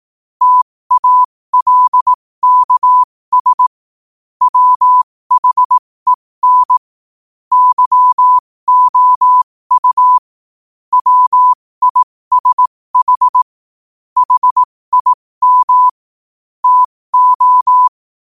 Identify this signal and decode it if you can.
Morse code, heard as a keyed tone, about 18 wpm, spaced as for 15 wpm, 1000 Hz. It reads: TALKS WHEN YOU WISH HIM TO
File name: qotdfiles / 15wpm